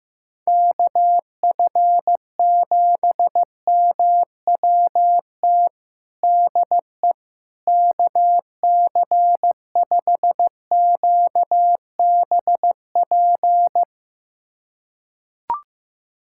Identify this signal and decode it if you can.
Morse code, heard as a keyed tone, about 16 wpm, 700 Hz. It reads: KF7MWT DE KC5QBP
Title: USA Callsign DE Callsign - 6 Characters - 15wpm